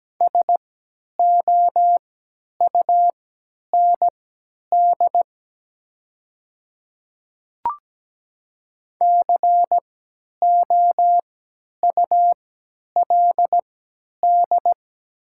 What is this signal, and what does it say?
Morse code, heard as a keyed tone, about 17 wpm, spaced as for 10 wpm, 690 Hz. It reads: SOUND COULD
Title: 3x Character Spacing - Top 100 Words - 17wpm